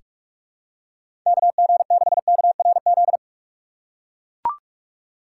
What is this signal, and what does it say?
Morse code, heard as a keyed tone, about 42 wpm, 700 Hz. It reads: KG6KRZ
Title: USA Call Signs - 6 Characters - 3x Character Spacing - 45wpm